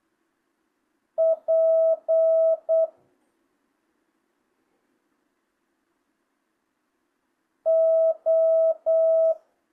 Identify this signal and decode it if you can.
Morse code, heard as a keyed tone, about 8 wpm, 650 Hz. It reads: P O
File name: jh8cbh / 2025cw